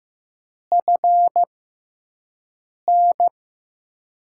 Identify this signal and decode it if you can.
Morse code, heard as a keyed tone, about 15 wpm, spaced as for 5 wpm, 700 Hz. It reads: FN
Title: USA Call Signs - 6 Characters - 6x Character Spacing - 15wpm